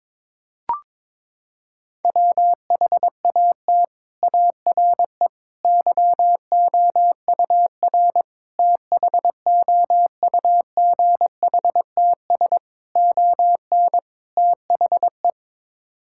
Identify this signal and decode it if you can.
Morse code, heard as a keyed tone, about 21 wpm, 700 Hz. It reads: WHAT ARE YOUR THOUGHTS ON THE
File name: practice / 22wpm / Sets of 6 Words - 22wpm